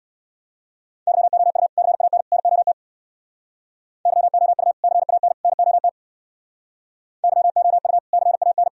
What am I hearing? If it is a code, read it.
Morse code, heard as a keyed tone, about 64 wpm, 700 Hz. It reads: YOU CAN NOT YOU CAN NOT YOU CAN